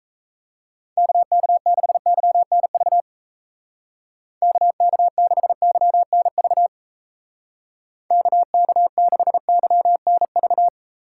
Morse code, heard as a keyed tone, about 34 wpm, 700 Hz. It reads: KK6YNV KK6YNV KK6YNV
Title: Speed Racing - USA 2x3 Call Signs 28wpm